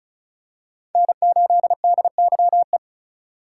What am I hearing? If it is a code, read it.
Morse code, heard as a keyed tone, about 36 wpm, 700 Hz. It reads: N8DYE
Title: USA Call Signs - 5 Characters - 2x Character Spacing - 35wpm